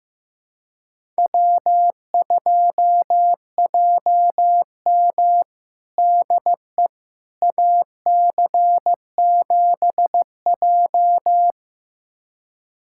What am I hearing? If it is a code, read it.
Morse code, heard as a keyed tone, about 15 wpm, 700 Hz. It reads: W2JM DE AC7J
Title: USA Callsign DE Callsign 15wpm